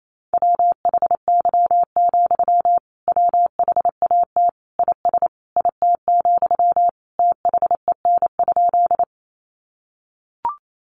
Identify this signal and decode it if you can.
Morse code, heard as a keyed tone, about 28 wpm, 700 Hz. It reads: WHY, WHAT IS IT, THEN?
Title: Sentences from Top 200 Words 28wpm